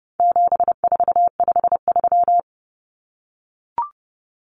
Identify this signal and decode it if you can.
Morse code, heard as a keyed tone, about 30 wpm, 700 Hz. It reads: 7453
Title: Numbers - 4-Digit 30wpm